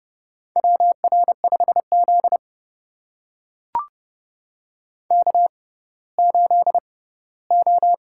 Morse code, heard as a keyed tone, about 31 wpm, spaced as for 17 wpm, 700 Hz. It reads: WR5Z K8O
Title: USA Call Signs - 4 Characters - 6x Character Spacing - 30wpm